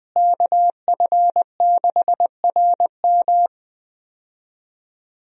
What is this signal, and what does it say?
Morse code, heard as a keyed tone, about 20 wpm, 700 Hz. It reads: KF6RM
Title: USA Call Signs - 5 Characters - Repeated 3x - 20wpm